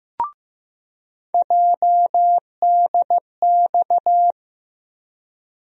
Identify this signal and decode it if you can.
Morse code, heard as a keyed tone, about 15 wpm, 700 Hz. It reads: JDX
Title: USA Call Sign Postfixes - 3 Letters 15wpm